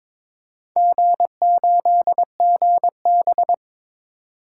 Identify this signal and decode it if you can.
Morse code, heard as a keyed tone, about 22 wpm, 700 Hz. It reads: G8GB
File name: Four Letter-Number Combinations - 22wpm